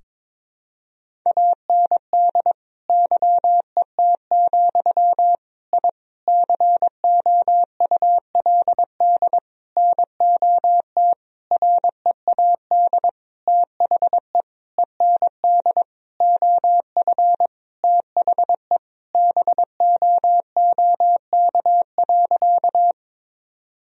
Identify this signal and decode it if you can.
Morse code, heard as a keyed tone, about 21 wpm, 700 Hz. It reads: AND YET, I COULD NOT READ THE END OF THE BOOK.
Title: Sentences from Top 1000 Words - Review Part 1 - 22wpm